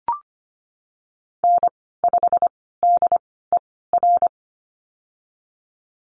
Morse code, heard as a keyed tone, about 25 wpm, spaced as for 16 wpm, 700 Hz. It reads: N5DER
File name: Lesson 15g - Number 5 - Call Signs 25wpm